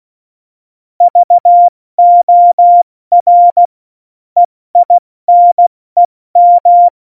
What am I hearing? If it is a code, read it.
Morse code, heard as a keyed tone, about 16 wpm, 700 Hz. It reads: VOR EINEM
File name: Morsekurs / Witze / Witz 28. August 2018, 70 F80